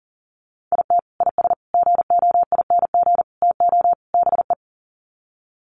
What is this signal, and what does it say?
Morse code, heard as a keyed tone, about 40 wpm, 700 Hz. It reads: IT IS GOING TO BE
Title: Sets of 5 Words 40wpm